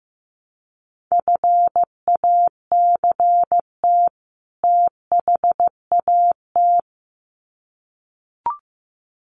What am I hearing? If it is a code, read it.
Morse code, heard as a keyed tone, about 15 wpm, 700 Hz. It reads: FACT THAT
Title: Sets of 2 Words 15wpm